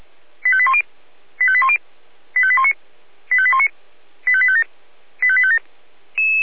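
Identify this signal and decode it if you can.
Five-tone selective call (ZVEI), heard as quick runs of five tones, tones of about 70 ms. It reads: 85810 85810 85819 85819 85857 85857
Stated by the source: ZVEI five-tone selective call received on SDR.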